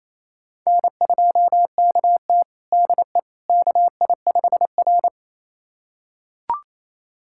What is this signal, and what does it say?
Morse code, heard as a keyed tone, about 28 wpm, 700 Hz. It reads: N2KT DE KI5R
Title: USA Callsign DE Callsign - 4 Characters - 28wpm